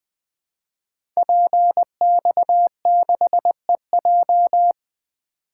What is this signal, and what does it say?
Morse code, heard as a keyed tone, about 20 wpm, 700 Hz. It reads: PX6EJ
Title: Code Groups - 5 Letter:Numbers - 20wpm